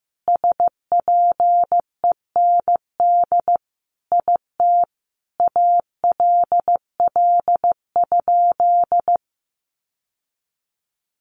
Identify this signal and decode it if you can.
Morse code, heard as a keyed tone, about 15 wpm, 700 Hz. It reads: SPEND IT ALL?
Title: Sentences from Top 800 Words 15wpm